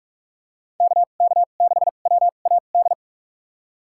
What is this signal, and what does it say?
Morse code, heard as a keyed tone, about 46 wpm, spaced as for 33 wpm, 700 Hz. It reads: KK6WAD
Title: USA Call Signs - 6 Characters - 2x Character Spacing - 45wpm